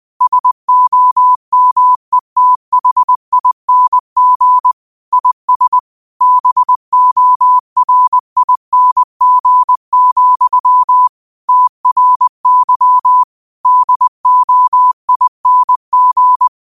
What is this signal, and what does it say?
Morse code, heard as a keyed tone, about 20 wpm, 1000 Hz. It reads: SOMETHING IS BORING, TRY DOING